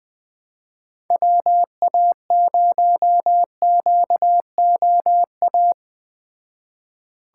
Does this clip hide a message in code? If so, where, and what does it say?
Yes: Morse code from 1.1 to 5.7 s, heard as a keyed tone, about 19 wpm, 700 Hz. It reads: WA0QOA